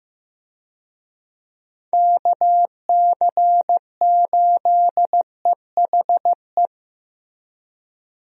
Morse code, heard as a keyed tone, about 15 wpm, 700 Hz. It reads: KC8EHE
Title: USA Call Signs - 6 Characters - Repeated 3x - 15wpm